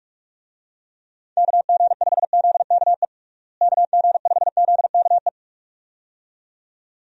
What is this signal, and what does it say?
Morse code, heard as a keyed tone, about 46 wpm, 700 Hz. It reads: KG5ZKE KG5ZKE
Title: USA Call Signs - 6 Characters - Repeated 2x - 45wpm